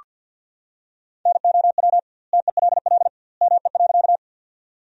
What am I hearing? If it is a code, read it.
Morse code, heard as a keyed tone, about 46 wpm, 700 Hz. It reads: NOW TELL ME.
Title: 2x Word-Spacing - Sentences from Top 200 Words - 50wpm